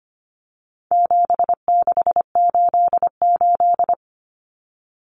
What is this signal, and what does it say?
Morse code, heard as a keyed tone, about 25 wpm, 700 Hz. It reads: 7688
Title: Common Cut Numbers - 4-Digit 25wpm